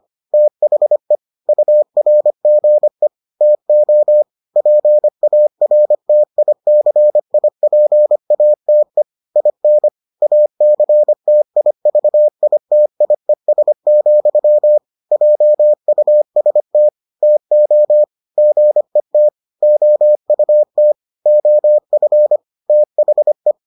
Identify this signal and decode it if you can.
Morse code, heard as a keyed tone, about 25 wpm, 600 Hz. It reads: THE URGE TO PARTICIPATE IN ACTIVITIES, JUST TO GET OUT OF THE